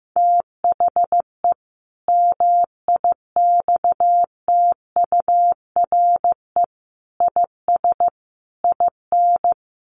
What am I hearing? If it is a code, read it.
Morse code, heard as a keyed tone, about 15 wpm, 700 Hz. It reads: THE MIXTURE IS IN
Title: Sentences from Top 1500 Words 15wpm